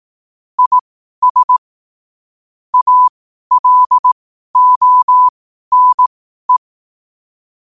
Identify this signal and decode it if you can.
Morse code, heard as a keyed tone, about 18 wpm, spaced as for 12 wpm, 1000 Hz. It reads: IS ALONE
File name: qotdfiles / 12wpm